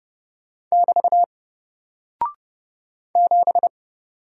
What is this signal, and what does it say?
Morse code, heard as a keyed tone, about 30 wpm, 700 Hz. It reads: <BT> 7
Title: Single Character (Letter:Number:Punctuation:Prosign) 30wpm